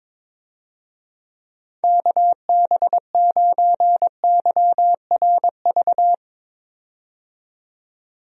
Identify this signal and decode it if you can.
Morse code, heard as a keyed tone, about 22 wpm, 700 Hz. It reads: KB9YRV
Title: USA Call Signs - 6 Characters - 22wpm